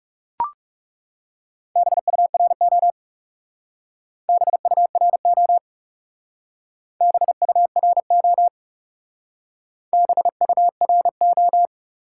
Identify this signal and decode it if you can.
Morse code, heard as a keyed tone, about 38 wpm, 700 Hz. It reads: BURO BURO BURO BURO